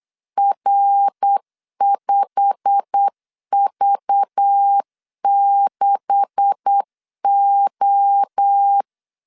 Morse code, heard as a keyed tone, about 8 wpm, 800 Hz. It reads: R5V6O
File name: LEZIONI / 9 2867 8wpm 3440